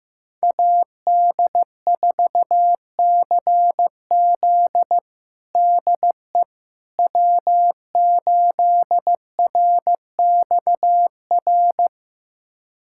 Morse code, heard as a keyed tone, about 15 wpm, 700 Hz. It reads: AD4CZ DE W8RXR
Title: USA Callsign DE Callsign - 5 Characters - 15wpm